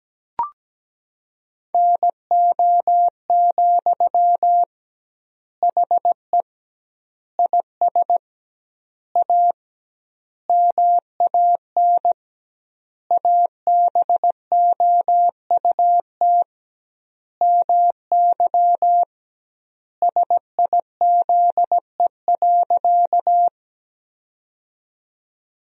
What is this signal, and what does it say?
Morse code, heard as a keyed tone, about 17 wpm, 700 Hz. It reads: NO, HE IS A MAN ABOUT MY SIZE.
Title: 2x Word-Spacing - Sentences from Top 400 Words - 17wpm